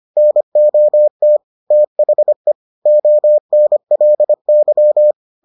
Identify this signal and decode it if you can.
Morse code, heard as a keyed tone, about 25 wpm, 600 Hz. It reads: NOT THE ONLY